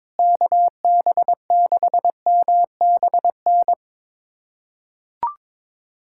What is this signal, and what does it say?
Morse code, heard as a keyed tone, about 22 wpm, 700 Hz. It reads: KB6MBN
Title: USA Callsign DE Callsign 22wpm